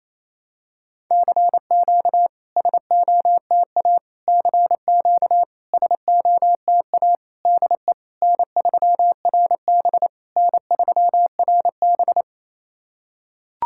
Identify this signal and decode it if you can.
Morse code, heard as a keyed tone, about 28 wpm, 700 Hz. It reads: CQ SOTA CQ SOTA DE N3RB N3RB
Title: SOTA QSO Practice 28wpm